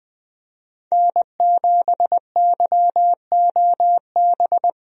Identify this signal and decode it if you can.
Morse code, heard as a keyed tone, about 21 wpm, 700 Hz. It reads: N7YOB
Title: USA Call Signs - 5 Characters - Repeated 2x - 20wpm